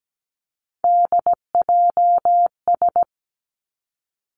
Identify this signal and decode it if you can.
Morse code, heard as a keyed tone, about 17 wpm, 700 Hz. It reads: DJS
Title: Three Challenging Letter-Number Combinations 17wpm